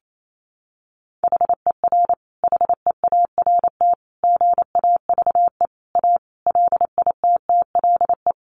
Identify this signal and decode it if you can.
Morse code, heard as a keyed tone, about 28 wpm, 700 Hz. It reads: HER HEART GAVE A LITTLE
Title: Sentences from Top 1300 Words 28wpm